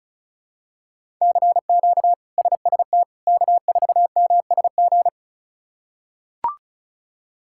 Morse code, heard as a keyed tone, about 36 wpm, 700 Hz. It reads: CQ SST K4MSG
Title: K1USN SST QSO Practice - 35wpm